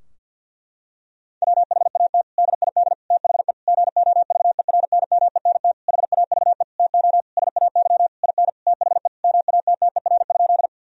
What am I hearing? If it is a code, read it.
Morse code, heard as a keyed tone, about 50 wpm, 700 Hz. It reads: WHAT DID THE GOVERNMENT HAVE TO SAY IN THE MATTER?
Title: Sentences from Top 1000 Words - Review Part 3 - 50wpm